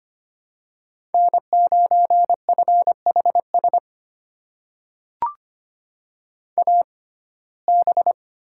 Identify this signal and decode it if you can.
Morse code, heard as a keyed tone, about 25 wpm, spaced as for 18 wpm, 700 Hz. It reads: N9FHS AB
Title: USA Call Signs - 5 Characters - 6x Character Spacing - 25wpm